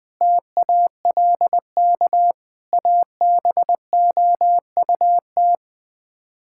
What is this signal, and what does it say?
Morse code, heard as a keyed tone, about 19 wpm, 700 Hz. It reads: TALK ABOUT
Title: Sets of 8 Words - 20wpm